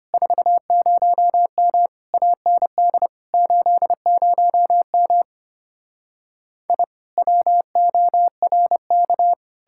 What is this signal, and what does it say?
Morse code, heard as a keyed tone, about 33 wpm, 700 Hz. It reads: 40M AND 80M I WORK